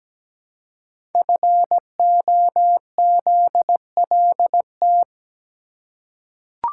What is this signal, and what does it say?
Morse code, heard as a keyed tone, about 17 wpm, 700 Hz. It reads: FOZLT
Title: Code Groups - 5 Letters - 17wpm